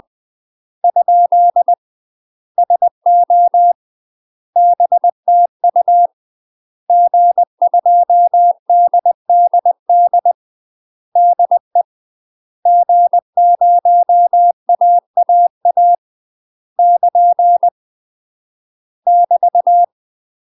Morse code, heard as a keyed tone, about 20 wpm, 700 Hz. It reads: ? SO BTU G2DDD DE G0AAA <KN> <BT>
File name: UK MASTER QSOs TRAINING/20WPM